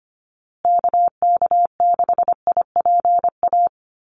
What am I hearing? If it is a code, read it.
Morse code, heard as a keyed tone, about 25 wpm, 700 Hz. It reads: KK6IPA
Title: USA Call Signs Repeated 2x 25wpm